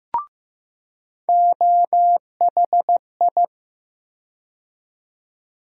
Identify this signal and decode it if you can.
Morse code, heard as a keyed tone, about 15 wpm, 700 Hz. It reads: OHI
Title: USA Call Sign Postfixes - 3 Letters 15wpm